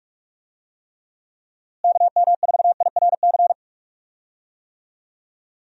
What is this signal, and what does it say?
Morse code, heard as a keyed tone, about 46 wpm, 700 Hz. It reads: KM4IRC
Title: USA Call Signs - 6 Characters - Repeated 3x - 45wpm